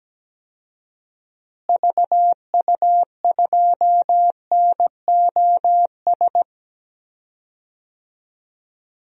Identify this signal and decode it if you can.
Morse code, heard as a keyed tone, about 17 wpm, 700 Hz. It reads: VU2NOS
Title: DXCC-100 Call Signs - 6 Characters - Repeated 2x - 17wpm